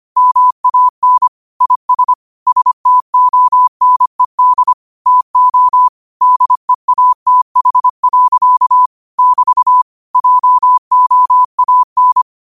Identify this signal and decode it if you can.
Morse code, heard as a keyed tone, about 25 wpm, 1000 Hz. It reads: MAN IS STONED TO DEATH. <BT> JOAN